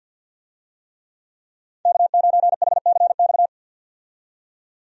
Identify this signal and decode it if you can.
Morse code, heard as a keyed tone, about 46 wpm, 700 Hz. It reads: K9HCX
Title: USA Call Signs - 5 Characters - 50wpm